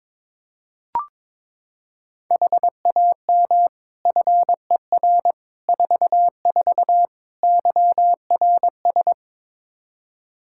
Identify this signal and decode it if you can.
Morse code, heard as a keyed tone, about 22 wpm, 700 Hz. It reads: HAM FER 44 YRS
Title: K1USN SST QSO Practice 22wpm